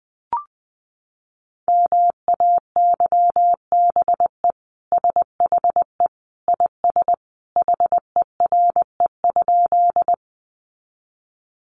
Morse code, heard as a keyed tone, about 20 wpm, 700 Hz. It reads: MAYBE SHE IS HERE?
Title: Sentences from Top 1000 Words 20wpm